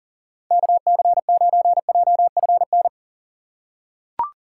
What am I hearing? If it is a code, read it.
Morse code, heard as a keyed tone, about 38 wpm, 700 Hz. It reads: KC9JFN
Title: USA Call Signs - 6 Characters - 3x Character Spacing - 40wpm